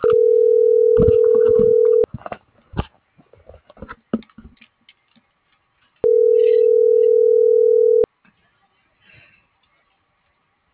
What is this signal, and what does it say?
Ringback tone (North America), 440 + 480 Hz.